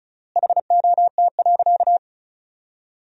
Morse code, heard as a keyed tone, about 36 wpm, 700 Hz. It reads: HOT.
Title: 3x Word-Spacing - Sentences from Top 500 Words - 35wpm